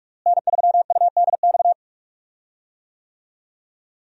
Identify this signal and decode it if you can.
Morse code, heard as a keyed tone, about 46 wpm, 700 Hz. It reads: N3UDX
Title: USA Call Signs - 5 Characters - Repeated 3x - 45wpm